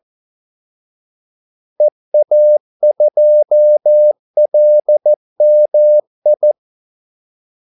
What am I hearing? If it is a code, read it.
Morse code, heard as a keyed tone, about 14 wpm, 600 Hz. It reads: EA2LMI
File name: Callsigns-14wpm